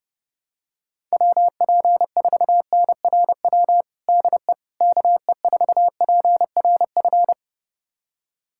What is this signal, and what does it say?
Morse code, heard as a keyed tone, about 30 wpm, 700 Hz. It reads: WP4NRW DE KE4PRF
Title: USA Callsign DE Callsign - 6 Characters - 30wpm